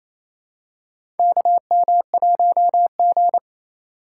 Morse code, heard as a keyed tone, about 28 wpm, 700 Hz. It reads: KM1G
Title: USA Call Signs - 4 Characters - 2x Character Spacing - 28wpm